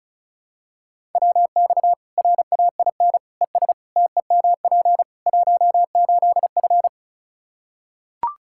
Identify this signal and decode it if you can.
Morse code, heard as a keyed tone, about 34 wpm, 700 Hz. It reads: WX RAIN ES TEMP 18F
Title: K1USN SST QSO Practice 35wpm